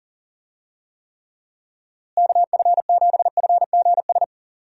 Morse code, heard as a keyed tone, about 40 wpm, 700 Hz. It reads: KF7FGS
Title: USA Call Signs - 6 Characters - Repeated 3x - 40wpm